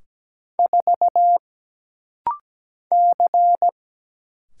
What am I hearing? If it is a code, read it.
Morse code, heard as a keyed tone, about 17 wpm, 700 Hz. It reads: 4 C
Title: Single Letter-Number - Rapid-Fire 17wpm